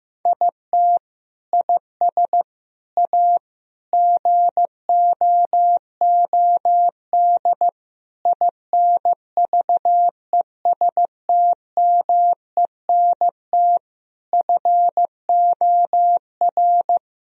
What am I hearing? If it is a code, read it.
Morse code, heard as a keyed tone, about 15 wpm, 700 Hz. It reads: IT IS A GOOD INVESTMENT FOR
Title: Sentences from Top 1000 Words - Review Part 4 - 15wpm